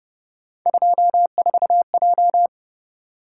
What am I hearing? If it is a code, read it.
Morse code, heard as a keyed tone, about 30 wpm, 700 Hz. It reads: 24J